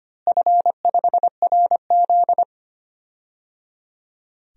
Morse code, heard as a keyed tone, about 25 wpm, 700 Hz. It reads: F5RZ
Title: DXCC-100 Call Signs - 4 Characters - Repeated 3x - 25wpm